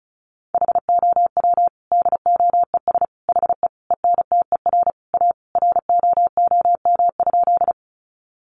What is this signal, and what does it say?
Morse code, heard as a keyed tone, about 34 wpm, 700 Hz. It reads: HOW DOES HE ENTER A ROOM?
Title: Sentences from Top 1300 Words 35wpm